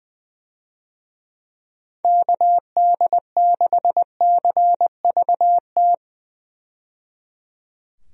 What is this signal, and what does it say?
Morse code, heard as a keyed tone, about 20 wpm, 700 Hz. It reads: KD6CVT
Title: USA Call Signs - 6 Characters - 20wpm